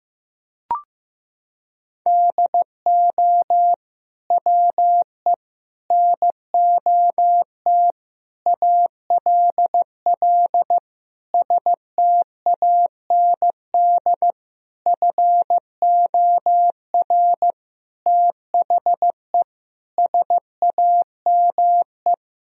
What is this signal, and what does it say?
Morse code, heard as a keyed tone, about 15 wpm, 700 Hz. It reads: DO WE NOT ALL STAND FOR THE SAME
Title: Sentences from Top 1000 Words - Review Part 3 - 15wpm